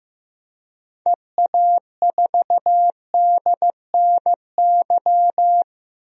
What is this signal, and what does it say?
Morse code, heard as a keyed tone, about 15 wpm, 700 Hz. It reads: EA4DNY
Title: DXCC-100 Call Signs - 6 Characters - 15wpm